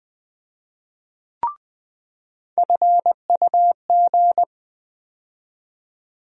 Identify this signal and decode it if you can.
Morse code, heard as a keyed tone, about 20 wpm, 700 Hz. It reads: FUG
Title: USA Call Sign Postfixes - 3 Letters 20wpm